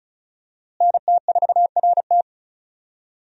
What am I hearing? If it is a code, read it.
Morse code, heard as a keyed tone, about 36 wpm, 700 Hz. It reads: NT4RT
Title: USA Call Signs - 5 Characters - Repeated 3x - 35wpm